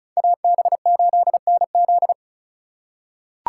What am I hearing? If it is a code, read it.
Morse code, heard as a keyed tone, about 36 wpm, 700 Hz. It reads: AB8NZ
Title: USA Call Signs - 5 Characters - 3x Character Spacing - 35wpm